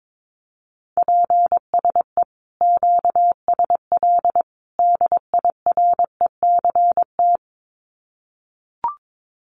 Morse code, heard as a keyed tone, about 22 wpm, 700 Hz. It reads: PSE QSL DIRECT
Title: Common QSO Phrases 22wpm